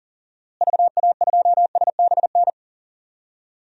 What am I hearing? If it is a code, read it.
Morse code, heard as a keyed tone, about 40 wpm, 700 Hz. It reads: VA2SBN
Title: DXCC-100 Call Signs - 6 Characters - Repeated 2x - 40wpm